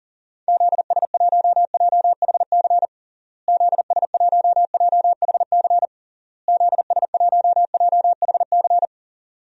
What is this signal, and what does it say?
Morse code, heard as a keyed tone, about 40 wpm, 700 Hz. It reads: ZS1JHC ZS1JHC ZS1JHC
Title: DXCC-100 Call Signs - 6 Characters - Repeated 3x - 40wpm